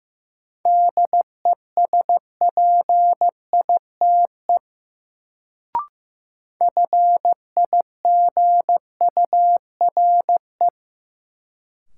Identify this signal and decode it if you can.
Morse code, heard as a keyed tone, about 15 wpm, 700 Hz. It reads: DESPITE FIGURE
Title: Sets of 4 Words - New Word Drill - 15wpm